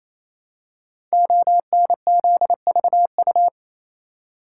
Morse code, heard as a keyed tone, about 28 wpm, 700 Hz. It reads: ONZVU
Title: Code Groups - 5 Letter:Numbers - 28wpm